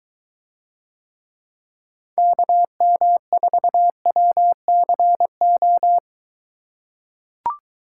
Morse code, heard as a keyed tone, about 22 wpm, 700 Hz. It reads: KM4WCO